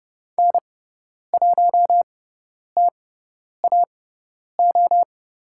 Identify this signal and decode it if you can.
Morse code, heard as a keyed tone, about 30 wpm, spaced as for 10 wpm, 700 Hz. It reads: N1TAO
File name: Lesson 18g - Letter m - Call Signs 30wpm